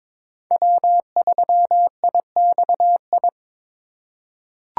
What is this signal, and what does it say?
Morse code, heard as a keyed tone, about 22 wpm, 700 Hz. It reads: W3IXI
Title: USA Call Signs - 5 Characters - 2x Character Spacing - 22wpm